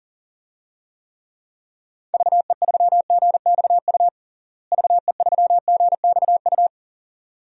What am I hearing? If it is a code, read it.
Morse code, heard as a keyed tone, about 40 wpm, 700 Hz. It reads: VE3GXU VE3GXU
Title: DXCC-100 Call Signs - 6 Characters - Repeated 2x - 40wpm